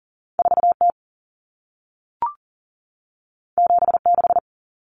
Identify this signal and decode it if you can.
Morse code, heard as a keyed tone, about 40 wpm, 700 Hz. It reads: 4T 76
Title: Common Cut Numbers - 2-Digit 40wpm